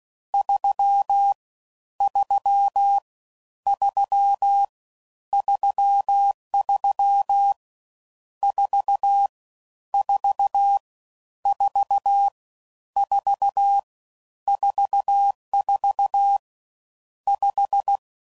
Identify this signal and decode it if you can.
Morse code, heard as a keyed tone, about 16 wpm, 800 Hz. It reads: 3 3 3 33 4 4 4 4 44 5
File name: Mixed / Misc / Numbs